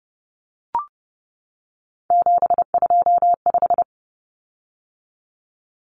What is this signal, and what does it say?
Morse code, heard as a keyed tone, about 30 wpm, 700 Hz. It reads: 725
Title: Common Cut Numbers - 3-Digit 30wpm